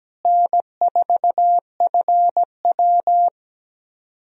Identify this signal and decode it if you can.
Morse code, heard as a keyed tone, about 17 wpm, 700 Hz. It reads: N4FW
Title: USA Call Signs - 4 Characters - Repeated 3x - 17wpm